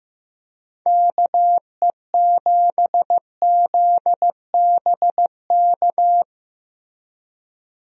Morse code, heard as a keyed tone, about 15 wpm, 700 Hz. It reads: KE7ZBK
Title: USA Call Signs - 6 Characters - Repeated 3x - 15wpm